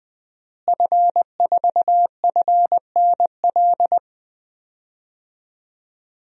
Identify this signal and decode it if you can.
Morse code, heard as a keyed tone, about 20 wpm, 700 Hz. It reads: F4FNL
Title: DXCC-100 Call Signs - 5 Characters - Repeated 3x - 20wpm